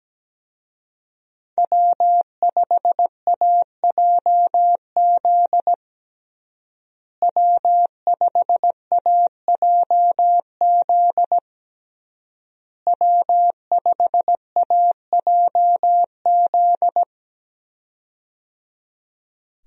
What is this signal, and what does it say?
Morse code, heard as a keyed tone, about 17 wpm, 700 Hz. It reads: W5AJZ W5AJZ W5AJZ
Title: USA Call Signs - 5 Characters - Repeated 3x - 17wpm